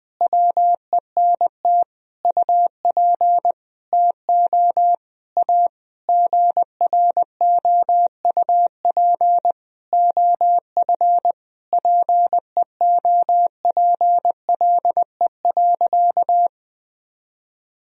Morse code, heard as a keyed tone, about 20 wpm, 700 Hz. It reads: WENT UP TO A GROUP OF PEOPLE.
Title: Sentences from Top 1000 Words - Review Part 1 - 20wpm